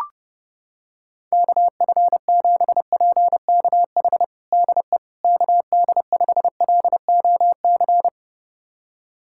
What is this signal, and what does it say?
Morse code, heard as a keyed tone, about 30 wpm, 700 Hz. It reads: KF7PKH DE KD5LOC
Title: USA Callsign DE Callsign 30wpm